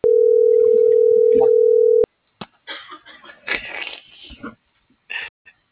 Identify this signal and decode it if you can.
Ringback tone (North America), 440 + 480 Hz.